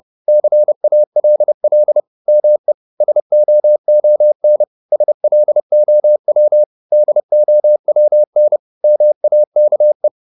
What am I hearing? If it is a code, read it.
Morse code, heard as a keyed tone, about 30 wpm, 600 Hz. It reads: CALL ME SOON SLOW DOWN MAKE